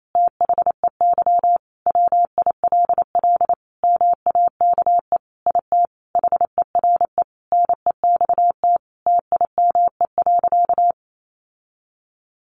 Morse code, heard as a keyed tone, about 28 wpm, 700 Hz. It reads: THEY WILL MAKE IT HERE NEXT TIME.
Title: Sentences from Top 1500 Words 28wpm